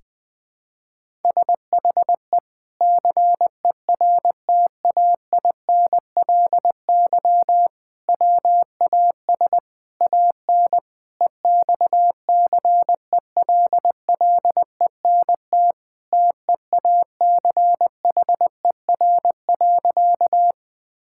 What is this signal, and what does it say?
Morse code, heard as a keyed tone, about 21 wpm, 700 Hz. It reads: SHE CERTAINLY WAS AN EXCELLENT TEACHER.
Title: Sentences from Top 1000 Words - Review Part 6 - 20wpm